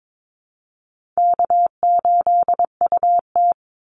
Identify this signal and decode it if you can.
Morse code, heard as a keyed tone, about 22 wpm, 700 Hz. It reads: K8UT
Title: Easy Call Signs Repeated 3x - 22wpm